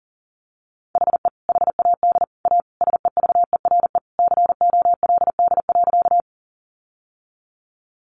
Morse code, heard as a keyed tone, about 40 wpm, 700 Hz. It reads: HE HAD A SEVERE COLD.
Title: Sentences from Top 1400 Words 40wpm